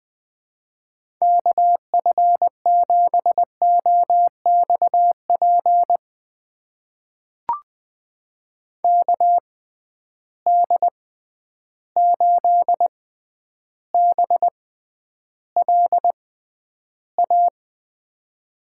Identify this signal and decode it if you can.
Morse code, heard as a keyed tone, about 20 wpm, spaced as for 10 wpm, 700 Hz. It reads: KF7OXP KD8BLA
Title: USA Call Signs - 6 Characters - 6x Character Spacing - 20wpm